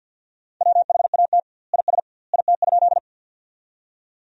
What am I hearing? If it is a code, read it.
Morse code, heard as a keyed tone, about 47 wpm, 700 Hz. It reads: WHAT IS IT?
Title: 2x Word-Spacing - Sentences from Top 100 Words - 50wpm